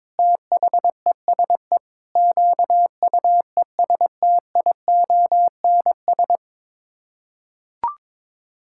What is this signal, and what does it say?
Morse code, heard as a keyed tone, about 22 wpm, 700 Hz. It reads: THESE QUESTIONS
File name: Sets of 9 Words - 22wpm